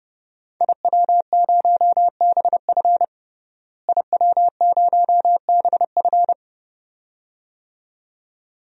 Morse code, heard as a keyed tone, about 30 wpm, 700 Hz. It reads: IW0BF IW0BF
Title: DXCC-100 Call Signs - 5 Characters - Repeated 2x - 30wpm